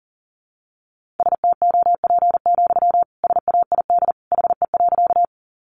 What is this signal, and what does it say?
Morse code, heard as a keyed tone, about 40 wpm, 700 Hz. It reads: STOP, SAID HE.
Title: Sentences from Top 300 Words 40wpm